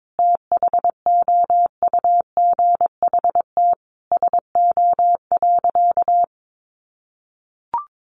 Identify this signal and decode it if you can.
Morse code, heard as a keyed tone, about 22 wpm, 700 Hz. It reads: THOUGHT SO.
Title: Sentences from Top 1400 Words 22wpm